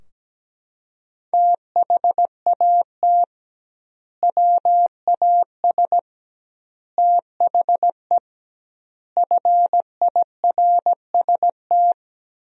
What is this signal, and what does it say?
Morse code, heard as a keyed tone, about 17 wpm, 700 Hz. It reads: THAT WAS THE FIRST
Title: 2x Word-Spacing - Sentences from Top 300 Words - 17wpm